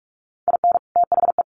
Morse code, heard as a keyed tone, about 44 wpm, 700 Hz. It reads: IN THE
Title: Sets of 2 Words 45wpm